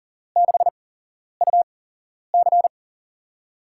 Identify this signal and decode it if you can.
Morse code, heard as a keyed tone, about 39 wpm, spaced as for 11 wpm, 700 Hz. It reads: 6UC
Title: USA Call Signs - 5 Characters - 8x Character Spacing - 40wpm